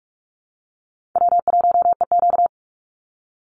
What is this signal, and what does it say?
Morse code, heard as a keyed tone, about 44 wpm, 700 Hz. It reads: W1EQ
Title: Easy Call Signs - 45wpm